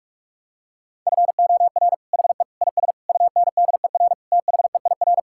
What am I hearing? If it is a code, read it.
Morse code, heard as a keyed tone, about 46 wpm, 700 Hz. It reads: FOR HE IS UNDER THEIR
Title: Sentences from Top 1600 Words 45wpm